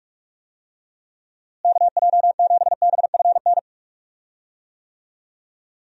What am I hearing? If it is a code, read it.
Morse code, heard as a keyed tone, about 42 wpm, 700 Hz. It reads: KJ7BFN
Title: USA Call Signs - 6 Characters - Repeated 3x - 45wpm